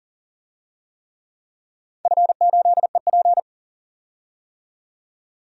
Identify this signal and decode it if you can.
Morse code, heard as a keyed tone, about 40 wpm, 700 Hz. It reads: F8EP
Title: DXCC-100 Call Signs - 4 Characters - 40wpm